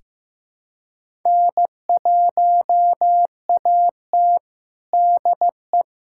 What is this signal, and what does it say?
Morse code, heard as a keyed tone, about 15 wpm, 700 Hz. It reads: N1AT DE
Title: USA Callsign DE Callsign - 4 Characters - 15wpm